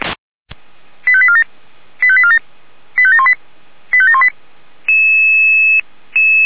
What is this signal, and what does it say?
Five-tone selective call (ZVEI), heard as quick runs of five tones, tones of about 70 ms. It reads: 85847 85847 85718 85718